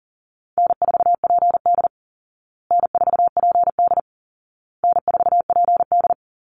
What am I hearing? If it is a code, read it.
Morse code, heard as a keyed tone, about 40 wpm, 700 Hz. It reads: N4PD N4PD N4PD
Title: Easy Call Signs Repeated 3x - 40wpm